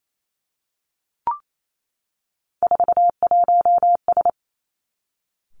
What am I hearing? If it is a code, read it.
Morse code, heard as a keyed tone, about 28 wpm, 700 Hz. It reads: 41S
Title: Three Challenging Letter-Number Combinations 28wpm